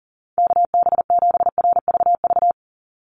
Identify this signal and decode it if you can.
Morse code, heard as a keyed tone, about 40 wpm, 700 Hz. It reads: KB7RVV
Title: USA Call Signs 40wpm